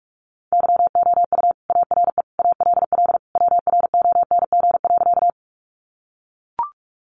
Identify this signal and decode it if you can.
Morse code, heard as a keyed tone, about 44 wpm, 700 Hz. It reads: YOU ARE ALL WRONG.
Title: Sentences from Top 900 Words 45wpm